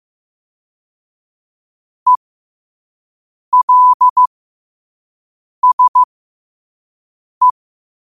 Morse code, heard as a keyed tone, about 15 wpm, spaced as for 5 wpm, 1000 Hz. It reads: ELSE
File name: qotdfiles / 5wpm